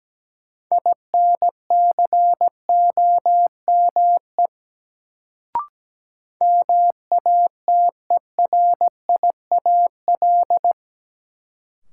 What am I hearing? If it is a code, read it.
Morse code, heard as a keyed tone, about 17 wpm, 700 Hz. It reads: INCOME MATERIAL
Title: Sentences from Top 1000 Words - Review Part 5 - New Word Drill - 17wpm